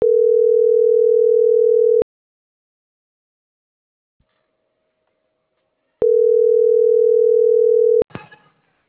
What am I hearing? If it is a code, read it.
Ringback tone (North America), 440 + 480 Hz.